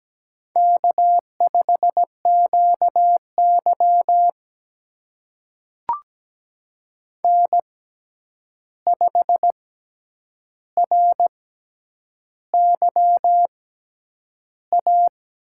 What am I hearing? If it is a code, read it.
Morse code, heard as a keyed tone, about 17 wpm, spaced as for 8 wpm, 700 Hz. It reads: K5QY N5RYA
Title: USA Call Signs - 6x Character Spacing - 17wpm